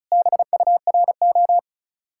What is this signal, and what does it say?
Morse code, heard as a keyed tone, about 43 wpm, 700 Hz. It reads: BURO
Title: Speed-Racing Common QSO Phrases Encore - 35wpm